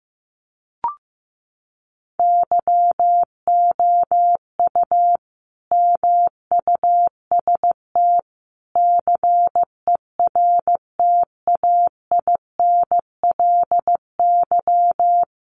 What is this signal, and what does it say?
Morse code, heard as a keyed tone, about 15 wpm, 700 Hz. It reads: YOU MUST CERTAINLY
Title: Sentences from Top 900 Words 15wpm